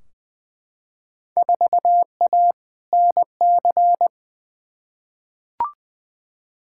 Morse code, heard as a keyed tone, about 20 wpm, 700 Hz. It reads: 4A NC
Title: Field Day Exchanges - 20wpm